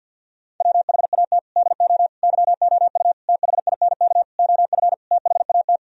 Morse code, heard as a keyed tone, about 52 wpm, 700 Hz. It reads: WHAT DO YOU THINK OF THAT
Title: Sentences from Top 1000 Words - Review Part 2 - 50wpm